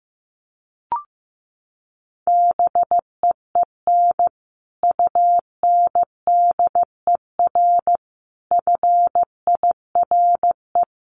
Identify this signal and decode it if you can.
Morse code, heard as a keyed tone, about 15 wpm, 700 Hz. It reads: BEEN UNDER FIRE
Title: Sentences from Top 700 Words 15wpm